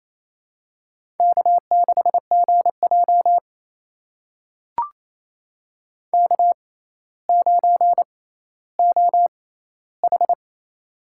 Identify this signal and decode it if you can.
Morse code, heard as a keyed tone, about 28 wpm, spaced as for 14 wpm, 700 Hz. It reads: K6GJ K9OH
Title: USA Call Signs - 4 Characters - 6x Character Spacing - 28wpm